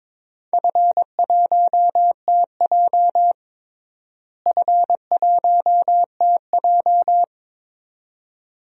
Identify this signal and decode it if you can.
Morse code, heard as a keyed tone, about 22 wpm, 700 Hz. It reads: F1TJ F1TJ
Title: DXCC-100 Call Signs - 4 Characters - Repeated 2x - 22wpm